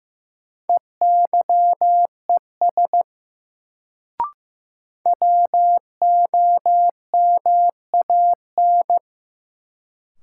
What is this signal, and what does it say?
Morse code, heard as a keyed tone, about 15 wpm, 700 Hz. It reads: EYES WOMAN
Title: Sets of 5 Words - New Word Drill - 15wpm